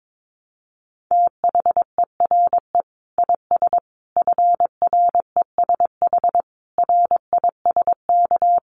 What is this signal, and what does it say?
Morse code, heard as a keyed tone, about 22 wpm, 700 Hz. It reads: THERE IS FRESH RISK
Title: Sentences from Top 1200 Words 22wpm